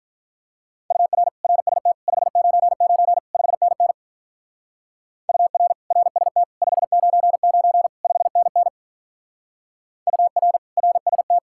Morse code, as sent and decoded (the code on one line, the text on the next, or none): ..- .-. / .-. ... - / ..... ----. ----. / ..... -. -. / ..- .-. / .-. ... - / ..... ----. ----. / ..... -. -. / ..- .-. / .-. ... -
UR RST 599 5NN UR RST 599 5NN UR RST